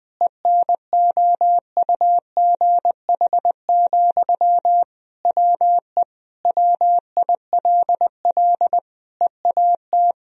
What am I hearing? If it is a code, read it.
Morse code, heard as a keyed tone, about 21 wpm, 700 Hz. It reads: ENOUGH, WE WILL EAT
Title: Sentences from Top 1000 Words - Review Part 4 - 20wpm